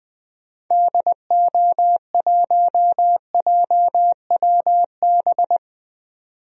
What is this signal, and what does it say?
Morse code, heard as a keyed tone, about 20 wpm, 700 Hz. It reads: DO1JWB
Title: DXCC-100 Call Signs - 6 Characters - 20wpm